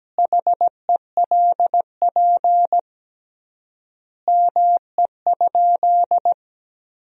Morse code, heard as a keyed tone, about 17 wpm, 700 Hz. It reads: HELP ME?
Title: 3x Word-Spacing - Sentences from Top 200 Words - 17wpm